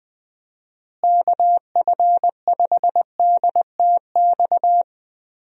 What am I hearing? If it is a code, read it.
Morse code, heard as a keyed tone, about 20 wpm, 700 Hz. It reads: KF5DTX
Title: USA Call Signs - 6 Characters - Repeated 3x - 20wpm